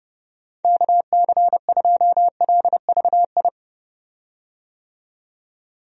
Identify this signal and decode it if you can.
Morse code, heard as a keyed tone, about 30 wpm, 700 Hz. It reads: KC2LVI
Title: USA Call Signs - 6 Characters - 30wpm